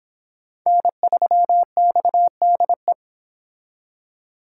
Morse code, heard as a keyed tone, about 26 wpm, 700 Hz. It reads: N3XDE